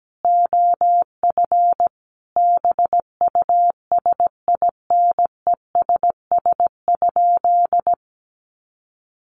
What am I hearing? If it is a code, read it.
Morse code, heard as a keyed tone, about 17 wpm, 700 Hz. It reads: OF BUSINESS?
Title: Sentences from Top 1300 Words 17wpm